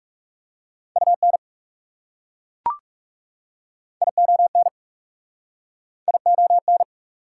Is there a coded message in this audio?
Yes: Morse code from 1.0 to 6.8 s, heard as a keyed tone, about 38 wpm, 690 Hz. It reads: UN ION ION